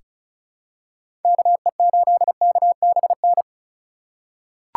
Morse code, heard as a keyed tone, about 36 wpm, 700 Hz. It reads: KE8KBN
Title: USA Call Signs - 4x Character Spacing - 35wpm